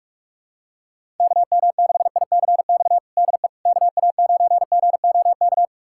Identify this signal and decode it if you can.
Morse code, heard as a keyed tone, about 41 wpm, 700 Hz. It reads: KM6ICX DE KA9GOK
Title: USA Callsign DE Callsign - 6 Characters - 45wpm